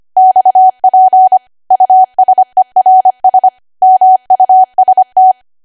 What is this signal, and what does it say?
Morse code, heard as a keyed tone, about 25 wpm, 750 Hz. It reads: XP USERS MUST
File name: Archive / 25 WPM / 25WPM